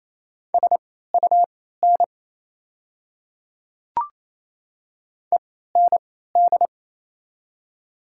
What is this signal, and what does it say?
Morse code, heard as a keyed tone, about 27 wpm, spaced as for 16 wpm, 700 Hz. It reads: SUN END
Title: 3x Character Spacing - Top 300 Words - 28wpm